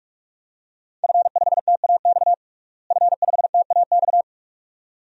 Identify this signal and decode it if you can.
Morse code, heard as a keyed tone, about 45 wpm, 700 Hz. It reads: F5TAX F5TAX
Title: DXCC-100 Call Signs - 5 Characters - Repeated 2x - 45wpm